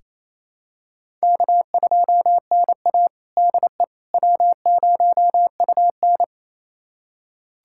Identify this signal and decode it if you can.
Morse code, heard as a keyed tone, about 28 wpm, 700 Hz. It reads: K2NA DE W0UN
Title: USA Callsign DE Callsign - 4 Characters - 28wpm